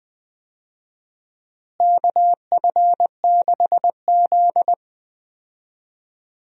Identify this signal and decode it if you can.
Morse code, heard as a keyed tone, about 20 wpm, 700 Hz. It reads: KF6Z
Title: USA Call Signs - 4 Characters - Repeated 2x - 20wpm